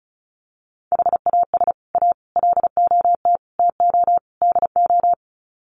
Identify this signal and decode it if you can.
Morse code, heard as a keyed tone, about 34 wpm, 700 Hz. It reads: HAS A LOT TO DO
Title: Sets of 5 Words Encore2 35wpm